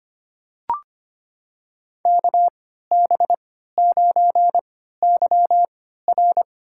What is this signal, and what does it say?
Morse code, heard as a keyed tone, about 25 wpm, spaced as for 14 wpm, 700 Hz. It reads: KB9YR
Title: USA Call Signs - 3x Character Spacing - 25wpm